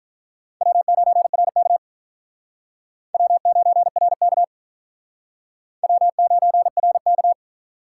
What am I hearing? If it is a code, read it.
Morse code, heard as a keyed tone, about 45 wpm, 700 Hz. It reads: W9RK W9RK W9RK